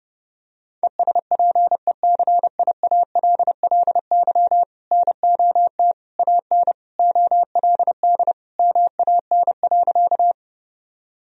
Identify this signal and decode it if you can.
Morse code, heard as a keyed tone, about 32 wpm, 700 Hz. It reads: ESPECIALLY NOT AN OLD MAN.
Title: Sentences from Top 1000 Words - Review Part 4 - 30wpm